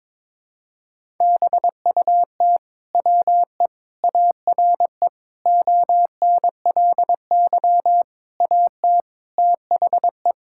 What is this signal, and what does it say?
Morse code, heard as a keyed tone, about 22 wpm, 700 Hz. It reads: BUT WE ARE ONLY AT THE
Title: Sentences from Top 1000 Words - Review Part 5 - 22wpm